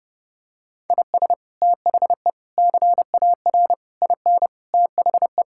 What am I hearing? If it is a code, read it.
Morse code, heard as a keyed tone, about 32 wpm, 700 Hz. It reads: IS THE CAR IN THE
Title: Sentences from Top 1600 Words 30wpm